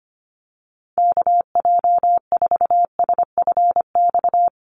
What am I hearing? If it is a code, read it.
Morse code, heard as a keyed tone, about 25 wpm, 690 Hz. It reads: KJ4SFX
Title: USA Call Signs 25wpm